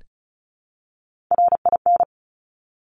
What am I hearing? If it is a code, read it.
Morse code, heard as a keyed tone, about 36 wpm, 700 Hz. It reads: RIN
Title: Top Three Letter Combinations in Words 35wpm